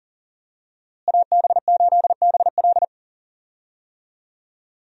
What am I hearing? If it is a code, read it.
Morse code, heard as a keyed tone, about 40 wpm, 700 Hz. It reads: AB8BL
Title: USA Call Signs - 5 Characters - Repeated 3x - 40wpm